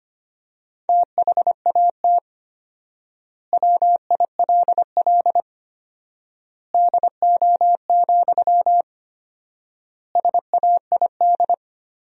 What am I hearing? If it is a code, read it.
Morse code, heard as a keyed tone, about 25 wpm, 700 Hz. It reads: THAT WILL DO, SAID
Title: 4x Word-Spacing - Sentences from Top 100 Words - 25wpm